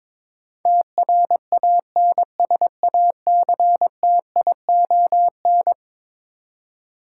Morse code, heard as a keyed tone, about 22 wpm, 700 Hz. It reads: TRANSACTION
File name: practice / 22wpm / Eleven Letter Words - 22wpm